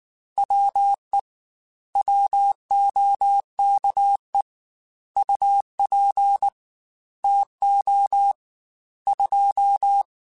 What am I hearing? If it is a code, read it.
Morse code, heard as a keyed tone, about 19 wpm, 800 Hz. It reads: WE WOKE UP TO 2
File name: Weather Sets / Weather Sets-mp3s / Weather 3 17 @18